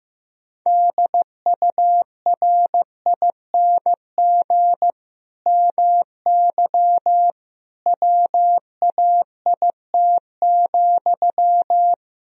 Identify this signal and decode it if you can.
Morse code, heard as a keyed tone, about 15 wpm, 700 Hz. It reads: DURING MY WAIT,
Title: Sentences from Top 1000 Words - Review Part 2 - 15wpm